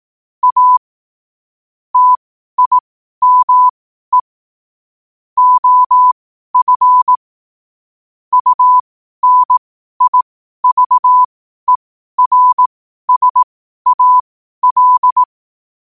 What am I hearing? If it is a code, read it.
Morse code, heard as a keyed tone, about 18 wpm, spaced as for 12 wpm, 1000 Hz. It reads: A TIME OF UNIVERSAL
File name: qotdfiles / 12wpm